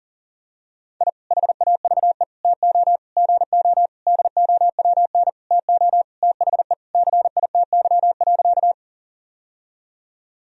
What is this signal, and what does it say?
Morse code, heard as a keyed tone, about 40 wpm, 700 Hz. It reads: I HAVE TO GO DOWN TO THE CITY.
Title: Sentences from Top 1000 Words - Review Part 1 - 40wpm